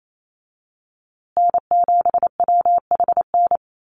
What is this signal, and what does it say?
Morse code, heard as a keyed tone, about 28 wpm, 700 Hz. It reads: N7WHN
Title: USA Call Signs Repeated 2x 28wpm